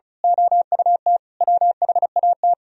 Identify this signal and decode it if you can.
Morse code, heard as a keyed tone, about 34 wpm, 700 Hz. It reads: OUT WHAT
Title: Speed-Racing - Sets of 5 Words 35wpm